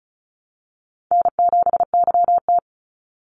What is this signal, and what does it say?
Morse code, heard as a keyed tone, about 34 wpm, 700 Hz. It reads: N7YT
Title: Easy Call Signs Repeated 2x - 35wpm